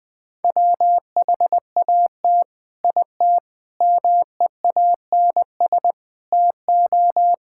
Morse code, heard as a keyed tone, about 20 wpm, 700 Hz. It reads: WHAT IT MEANS TO
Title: Build A Phrase - 5-Word Phrases - 20wpm